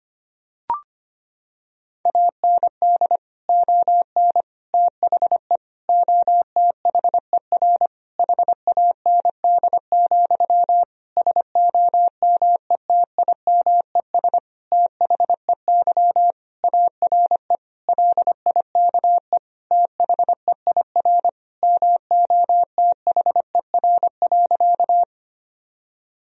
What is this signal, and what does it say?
Morse code, heard as a keyed tone, about 25 wpm, 700 Hz. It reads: AND ON THE OTHER HAND, SOMETIMES THEY ARE LIKE THEIR MOTHER.
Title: Sentences from Top 1000 Words - Review Part 3 - 25wpm